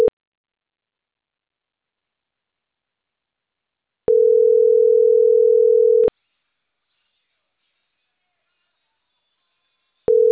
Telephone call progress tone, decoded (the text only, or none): ringback tone